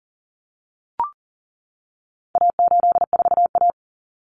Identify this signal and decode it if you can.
Morse code, heard as a keyed tone, about 40 wpm, 700 Hz. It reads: A84A
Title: Common Cut Numbers - 4-Digit 40wpm